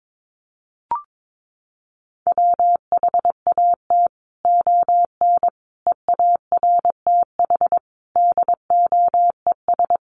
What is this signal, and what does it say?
Morse code, heard as a keyed tone, about 22 wpm, 700 Hz. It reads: WHAT ON EARTH DOES
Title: Sentences from Top 200 Words 22wpm